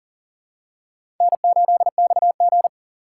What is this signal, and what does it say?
Morse code, heard as a keyed tone, about 40 wpm, 700 Hz. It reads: N8XG
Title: USA Call Signs - 4 Characters - 2x Character Spacing - 40wpm